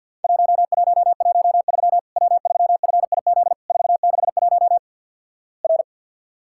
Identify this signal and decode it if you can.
Morse code, heard as a keyed tone, about 50 wpm, 700 Hz. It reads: 1113 W3FIZ 461 R
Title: Worked All Europe DX Contest - QTC Exchange - 50wpm